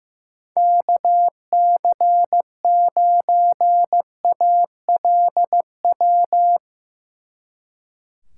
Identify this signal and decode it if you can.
Morse code, heard as a keyed tone, about 17 wpm, 700 Hz. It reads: KC9ALW
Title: Speed Racing - USA Call Signs 15wpm